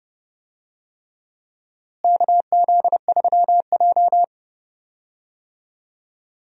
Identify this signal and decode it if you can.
Morse code, heard as a keyed tone, about 30 wpm, 700 Hz. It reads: KZ3J
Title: USA Call Signs - 4 Characters - Repeated 3x - 30wpm